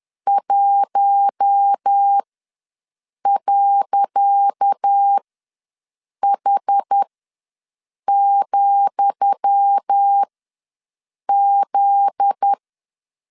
Morse code, heard as a keyed tone, about 10 wpm, spaced as for 6 wpm, 800 Hz. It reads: 1.H,Z
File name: LEZIONI / 10 SIGNS / 10D COMA ws 10wpm 4314